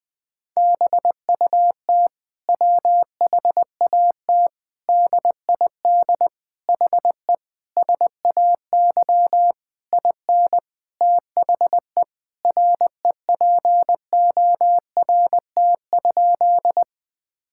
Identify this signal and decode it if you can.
Morse code, heard as a keyed tone, about 20 wpm, 700 Hz. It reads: BUT WHAT DID HE SAY IN THE REPORT?
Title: Sentences from Top 1000 Words - Review Part 6 - 20wpm